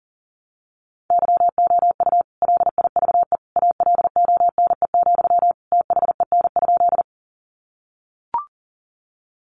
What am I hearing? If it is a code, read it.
Morse code, heard as a keyed tone, about 40 wpm, 700 Hz. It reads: YOU LIVE ALONE, THEN?
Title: Sentences from Top 1000 Words 40wpm